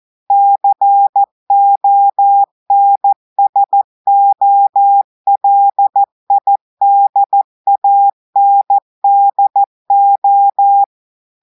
Morse code, heard as a keyed tone, about 14 wpm, 800 Hz. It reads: CONSOLIDANDO
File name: audios / portugais / portugais texte en clair 1 vitesse 14 mots minutes